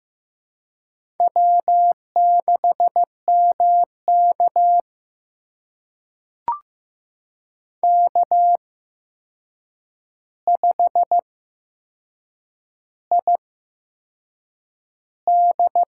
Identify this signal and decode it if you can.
Morse code, heard as a keyed tone, about 15 wpm, spaced as for 6 wpm, 700 Hz. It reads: W6MK K5ID
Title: USA Call Signs - 4 Characters - 8x Character Spacing - 15wpm